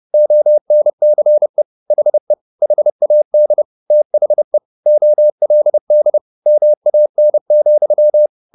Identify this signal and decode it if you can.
Morse code, heard as a keyed tone, about 30 wpm, 600 Hz. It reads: ONCE HE HAD THE OLD MAN,